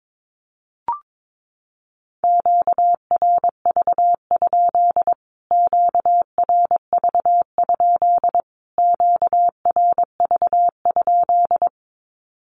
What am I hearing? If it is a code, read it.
Morse code, heard as a keyed tone, about 22 wpm, 700 Hz. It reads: QRV? QRV? QRV?
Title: Q-Codes Repeated 3x 22wpm